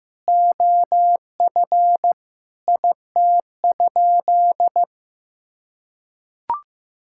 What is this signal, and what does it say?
Morse code, heard as a keyed tone, about 15 wpm, 700 Hz. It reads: OF IT?
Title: Sentences from Top 1000 Words - Review Part 5 - 15wpm